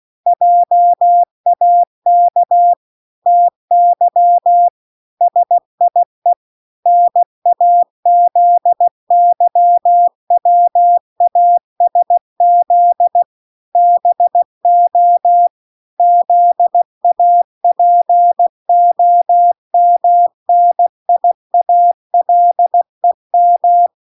Morse code, heard as a keyed tone, about 16 wpm, 700 Hz. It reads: JAK TY SIE NAZYWASZ BO ZAPOMNIALEM